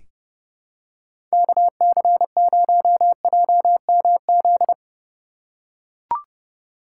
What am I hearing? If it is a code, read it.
Morse code, heard as a keyed tone, about 30 wpm, 700 Hz. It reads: KC0JMZ
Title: USA Call Signs - 6 Characters - 3x Character Spacing - 30wpm